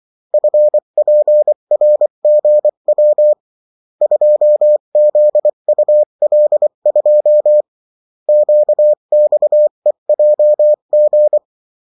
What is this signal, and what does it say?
Morse code, heard as a keyed tone, about 24 wpm, 600 Hz. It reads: FPRGW 2ZUL2 QXEJG
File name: morse mp3 / revision9-vitesse 22 mots